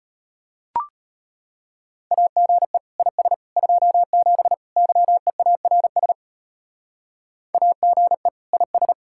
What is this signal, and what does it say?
Morse code, heard as a keyed tone, about 34 wpm, 700 Hz. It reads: AGE IS 27 YEARS AGE IS